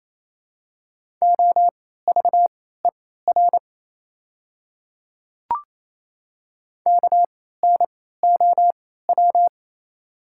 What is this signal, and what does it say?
Morse code, heard as a keyed tone, about 28 wpm, spaced as for 16 wpm, 700 Hz. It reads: OVER KNOW
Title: 3x Character Spacing - Top 100 Words - 28wpm